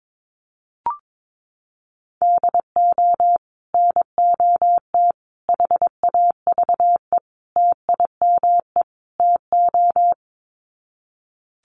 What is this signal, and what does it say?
Morse code, heard as a keyed tone, about 22 wpm, 700 Hz. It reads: DO NOT HAVE TIME TO
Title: Sets of 5 Words 22wpm